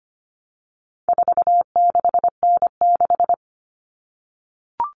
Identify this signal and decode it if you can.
Morse code, heard as a keyed tone, about 25 wpm, 700 Hz. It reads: 46N6
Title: Common Cut Numbers - 4-Digit 25wpm